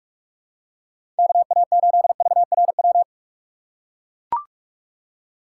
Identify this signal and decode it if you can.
Morse code, heard as a keyed tone, about 44 wpm, 700 Hz. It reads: KA8VRW
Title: USA Call Signs - 6 Characters - 2x Character Spacing - 45wpm